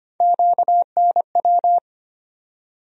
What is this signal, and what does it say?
Morse code, heard as a keyed tone, about 25 wpm, 700 Hz. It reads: QNW
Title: USA Call Sign Postfixes - 3 Letters 25wpm